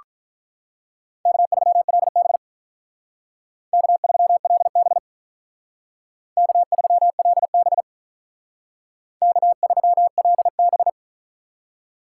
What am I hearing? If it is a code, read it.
Morse code, heard as a keyed tone, about 42 wpm, 700 Hz. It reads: K3LB K3LB K3LB K3LB
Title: Speed Racing - USA 1x2 Call Signs 35wpm